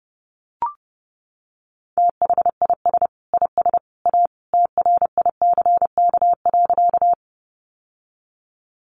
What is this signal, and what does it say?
Morse code, heard as a keyed tone, about 30 wpm, 700 Hz. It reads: THIS IS A TRICK.
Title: Sentences from Top 1500 Words 30wpm